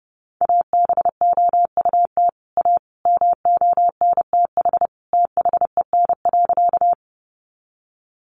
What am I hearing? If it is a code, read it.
Morse code, heard as a keyed tone, about 30 wpm, 700 Hz. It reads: ABOUT A MONTH THEN.
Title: Sentences from Top 800 Words 30wpm